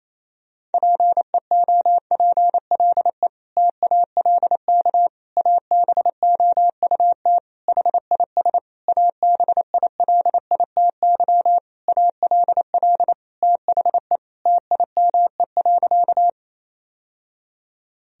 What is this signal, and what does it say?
Morse code, heard as a keyed tone, about 28 wpm, 700 Hz. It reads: PEOPLE TALK ABOUT HIS ABILITY ALL THE TIME.
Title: Sentences from Top 1000 Words - Review Part 3 - 28wpm